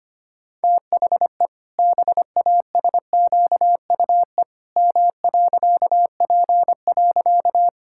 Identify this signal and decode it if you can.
Morse code, heard as a keyed tone, about 25 wpm, 700 Hz. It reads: THE BASQUE M.P.
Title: Pangram Challenge v2 - 25wpm